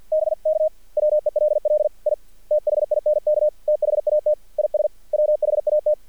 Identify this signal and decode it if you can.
Morse code, heard as a keyed tone, about 47 wpm, 610 Hz. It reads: BK WELL I THINK THAT IS WHAT
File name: archived / CRSnet